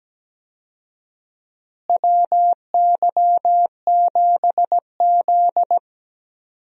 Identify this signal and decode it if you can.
Morse code, heard as a keyed tone, about 17 wpm, 700 Hz. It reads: WY7Z
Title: USA Call Signs - 4 Characters - Repeated 2x - 17wpm